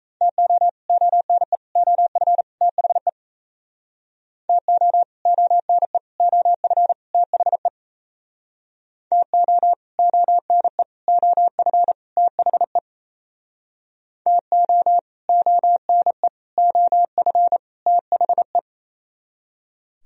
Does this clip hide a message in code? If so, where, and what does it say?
Yes: Morse code from 0.2 to 18.6 s, heard as a keyed tone, about 36 wpm, 700 Hz. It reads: TO ONE OF THE TO ONE OF THE TO ONE OF THE TO ONE OF THE